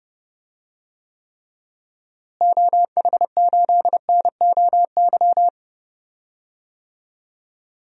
Morse code, heard as a keyed tone, about 30 wpm, 700 Hz. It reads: OH8NOY
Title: DXCC-100 Call Signs - 6 Characters - Repeated 3x - 30wpm